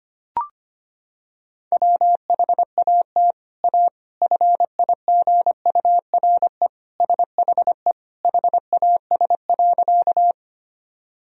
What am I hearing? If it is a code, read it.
Morse code, heard as a keyed tone, about 25 wpm, 700 Hz. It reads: WHAT A FIGURE SHE HAS.
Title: Sentences from Top 1000 Words - Review Part 2 - 25wpm